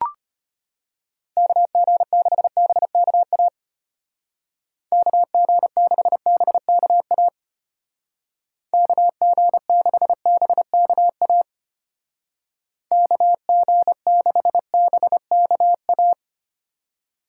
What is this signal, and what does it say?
Morse code, heard as a keyed tone, about 30 wpm, 700 Hz. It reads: KG6BKA KG6BKA KG6BKA KG6BKA
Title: Speed Racing - USA 2x3 Call Signs 25wpm